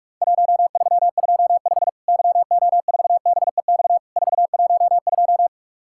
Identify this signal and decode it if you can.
Morse code, heard as a keyed tone, about 46 wpm, 700 Hz. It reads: 1325 YO4BEX 412
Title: Worked All Europe DX Contest - QTC Exchange - 45wpm